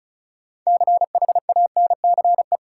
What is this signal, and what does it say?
Morse code, heard as a keyed tone, about 36 wpm, 700 Hz. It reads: CHANCE
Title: Sets of 3 Words - New Word Drill - 35wpm